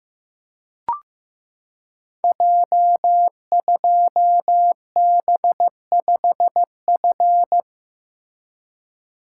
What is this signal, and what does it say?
Morse code, heard as a keyed tone, about 15 wpm, 700 Hz. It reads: J2B5F
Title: Code Groups - 5 Letter:Numbers - 15wpm